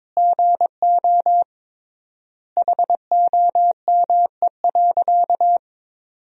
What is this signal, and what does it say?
Morse code, heard as a keyed tone, about 22 wpm, 700 Hz. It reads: GO HOME.
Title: 3x Word-Spacing - Sentences from Top 300 Words - 22wpm